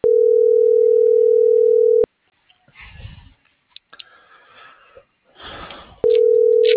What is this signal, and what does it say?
Ringback tone (North America), 440 + 480 Hz.
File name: CENTRAL MABE / out luis.palacios